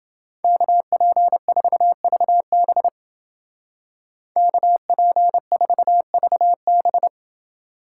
Morse code, heard as a keyed tone, about 28 wpm, 700 Hz. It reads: KP4VB KP4VB